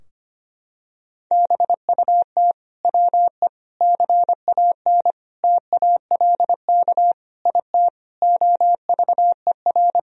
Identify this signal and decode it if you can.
Morse code, heard as a keyed tone, about 25 wpm, 700 Hz. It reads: BUT WE CAN TALK IT OVER
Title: Sentences from Top 1000 Words - Review Part 4 - 25wpm